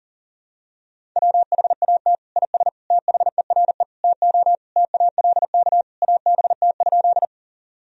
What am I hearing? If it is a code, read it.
Morse code, heard as a keyed tone, about 40 wpm, 700 Hz. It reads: WHAT IS THERE TO TALK ABT?
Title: Sentences from Top 400 Words and Abbreviations 40wpm